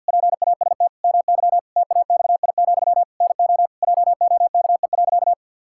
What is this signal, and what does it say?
Morse code, heard as a keyed tone, about 47 wpm, 700 Hz. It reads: PAST MY TAXI, NO JOKE.
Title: Pangram Challenge v2 - 50wpm